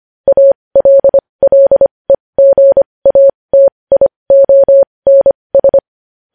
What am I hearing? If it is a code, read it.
Morse code, heard as a keyed tone, about 25 wpm, spaced as for 20 wpm, 550 Hz. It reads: ALLEGATIONS
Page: news headlines in morse code at 20 wpm